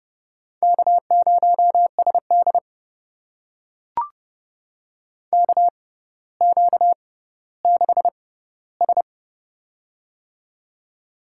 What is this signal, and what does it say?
Morse code, heard as a keyed tone, about 30 wpm, spaced as for 15 wpm, 690 Hz. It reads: K0SD KQ6S
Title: USA Call Signs - 4 Characters - 6x Character Spacing - 30wpm